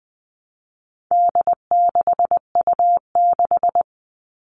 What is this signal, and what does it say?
Morse code, heard as a keyed tone, about 20 wpm, 700 Hz. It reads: D6U6
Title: All Cut Numbers - 4-Digit 20wpm